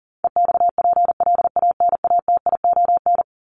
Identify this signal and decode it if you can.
Morse code, heard as a keyed tone, about 40 wpm, 700 Hz. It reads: EXPLANATION
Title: Top 1751-2000 Words 40wpm